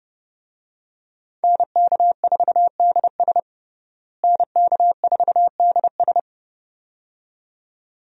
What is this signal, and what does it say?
Morse code, heard as a keyed tone, about 32 wpm, 700 Hz. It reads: NK4DS NK4DS
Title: USA Call Signs - 5 Characters - Repeated 2x - 30wpm